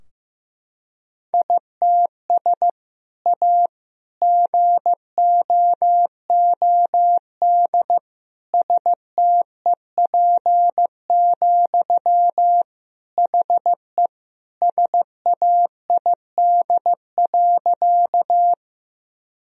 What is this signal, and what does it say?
Morse code, heard as a keyed tone, about 15 wpm, 700 Hz. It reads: ITS A GOOD STEP, HE SAID.
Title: Sentences from Top 1000 Words - Review Part 2 - 15wpm